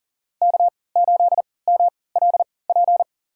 Morse code, heard as a keyed tone, about 37 wpm, spaced as for 22 wpm, 700 Hz. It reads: K8MLP
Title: USA Call Signs - 5 Characters - 3x Character Spacing - 40wpm